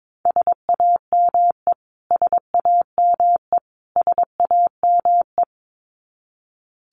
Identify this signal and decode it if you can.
Morse code, heard as a keyed tone, about 22 wpm, 700 Hz. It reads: SAME SAME SAME
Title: Top 500 Words Repeated 22wpm